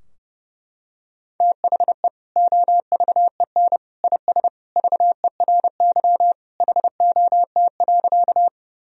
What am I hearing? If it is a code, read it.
Morse code, heard as a keyed tone, about 32 wpm, 700 Hz. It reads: THE OVEN IS VERY HOT.
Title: Sentences from Top 1000 Words - Review Part 4 - 30wpm